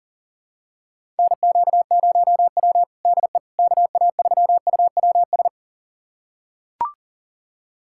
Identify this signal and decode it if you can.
Morse code, heard as a keyed tone, about 37 wpm, 700 Hz. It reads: NQ0W DE KA3UWS
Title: USA Callsign DE Callsign 40wpm